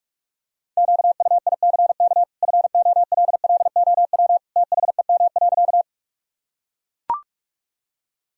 Morse code, heard as a keyed tone, about 44 wpm, 700 Hz. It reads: QUICK FOLLOW THEM.
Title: Sentences from Top 1000 Words - Review Part 1 - 45wpm